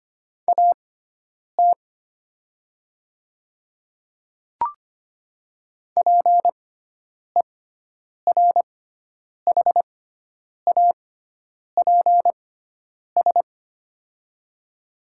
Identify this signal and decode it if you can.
Morse code, heard as a keyed tone, about 25 wpm, spaced as for 9 wpm, 700 Hz. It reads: AT PERHAPS
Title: 6x Character Spacing - Top 500 Words - 25wpm